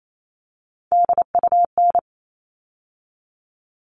Morse code, heard as a keyed tone, about 28 wpm, 700 Hz. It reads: DUN
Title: All Cut Numbers - 3-Digit 28wpm